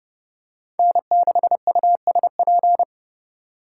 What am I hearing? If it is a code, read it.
Morse code, heard as a keyed tone, about 28 wpm, 700 Hz. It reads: N6USP